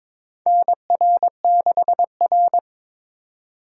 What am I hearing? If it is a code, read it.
Morse code, heard as a keyed tone, about 21 wpm, 700 Hz. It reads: NR6R
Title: USA Call Signs - 4 Characters - 2x Character Spacing - 22wpm